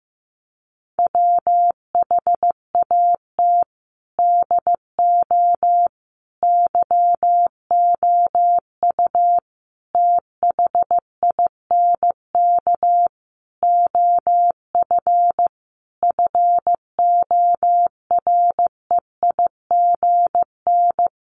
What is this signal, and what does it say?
Morse code, heard as a keyed tone, about 15 wpm, 700 Hz. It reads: WHAT DO YOU THINK OF FOREIGN
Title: Sentences from Top 1400 Words 15wpm